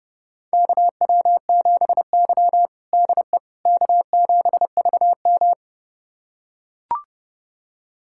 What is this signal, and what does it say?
Morse code, heard as a keyed tone, about 30 wpm, 700 Hz. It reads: KW7Y DE K7VM
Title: USA Callsign DE Callsign - 4 Characters - 30wpm